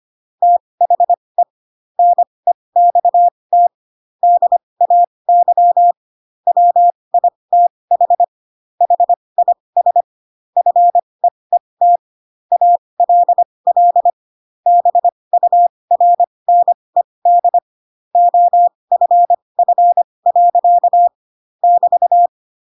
Morse code, heard as a keyed tone, about 25 wpm, spaced as for 20 wpm, 700 Hz. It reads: THE NEXT DAY WITH HIS FEET ALL BURNED OFF. <BT>
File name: Pinocchio / Pinocchio ch6 15 25 ik1hge